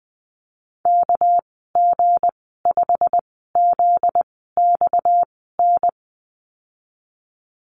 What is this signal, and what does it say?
Morse code, heard as a keyed tone, about 20 wpm, spaced as for 15 wpm, 700 Hz. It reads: KG5ZXN
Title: 245 Lesson 39g - Letter z - Call Signs 20wpm